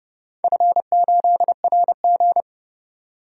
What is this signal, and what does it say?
Morse code, heard as a keyed tone, about 30 wpm, 700 Hz. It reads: F8RG
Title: DXCC-100 Call Signs - 4 Characters - 30wpm